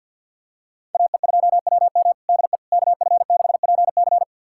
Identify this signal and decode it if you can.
Morse code, heard as a keyed tone, about 53 wpm, 700 Hz. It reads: AE2WM DE KF6PC
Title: USA Callsign DE Callsign - 5 Characters - 50wpm